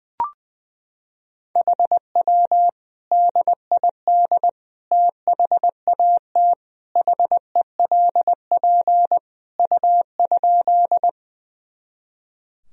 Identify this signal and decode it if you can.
Morse code, heard as a keyed tone, about 20 wpm, 700 Hz. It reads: HW DID THAT HELP U?
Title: Sentences from Top 200 Words and Abbreviations 20wpm